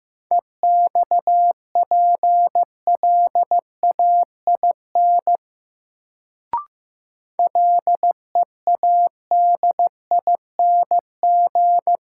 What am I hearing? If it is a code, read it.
Morse code, heard as a keyed tone, about 15 wpm, 700 Hz. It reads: EXPLAIN LEADING
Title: Sentences from Top 1100 Words - New Word Drill - 15wpm